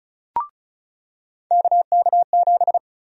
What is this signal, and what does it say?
Morse code, heard as a keyed tone, about 36 wpm, 700 Hz. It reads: KK7
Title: USA Call Sign Prefixes - 35wpm